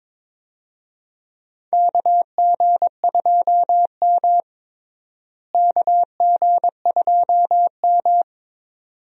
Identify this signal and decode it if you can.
Morse code, heard as a keyed tone, about 22 wpm, 700 Hz. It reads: KG2M KG2M
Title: USA Call Signs - 4 Characters - Repeated 2x - 22wpm